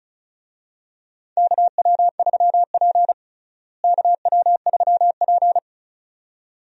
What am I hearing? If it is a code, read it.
Morse code, heard as a keyed tone, about 36 wpm, 700 Hz. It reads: KW3P KW3P
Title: USA Call Signs - 4 Characters - Repeated 2x - 35wpm